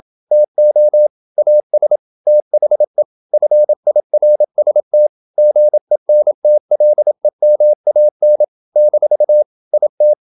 Morse code, heard as a keyed tone, about 27 wpm, 600 Hz. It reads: TO AS THE FIRST GENTLEMAN - IT
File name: Never 0027wpm